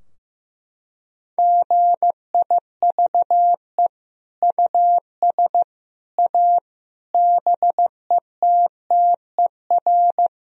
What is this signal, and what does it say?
Morse code, heard as a keyed tone, about 15 wpm, 700 Hz. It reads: GIVE US A BETTER
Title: Sentences from Top 1000 Words - Review Part 4 - 15wpm